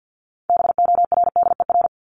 Morse code, heard as a keyed tone, about 48 wpm, 700 Hz. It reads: BORDER
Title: Top 1751-2000 Words 50wpm